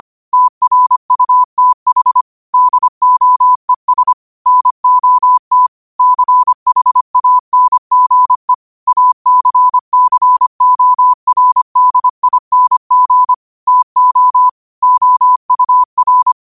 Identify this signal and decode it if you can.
Morse code, heard as a keyed tone, about 25 wpm, 1000 Hz. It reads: TRUTH DOES NOT CHANGE ACCORDING TO OUR
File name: qotdfiles / 25wpm